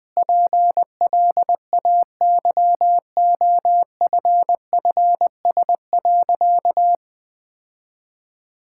Morse code, heard as a keyed tone, about 20 wpm, 700 Hz. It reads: PLAYOFFS.
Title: Sentences with 7-Word Phrases - 20wpm